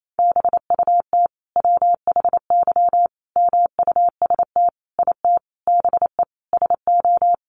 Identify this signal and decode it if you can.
Morse code, heard as a keyed tone, about 28 wpm, 700 Hz. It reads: BUT WHY MUST IT BE SO
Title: Sentences from Top 1300 Words 28wpm